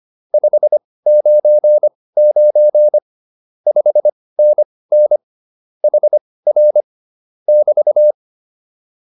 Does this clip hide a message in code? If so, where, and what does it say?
Yes: Morse code from 0.3 to 8.1 s, heard as a keyed tone, about 25 wpm, spaced as for 18 wpm, 600 Hz. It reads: 599 5NN HR <BT>